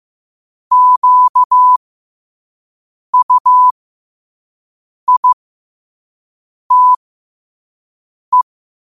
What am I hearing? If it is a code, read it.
Morse code, heard as a keyed tone, about 15 wpm, spaced as for 5 wpm, 1000 Hz. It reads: QUITE
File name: qotdfiles / 5wpm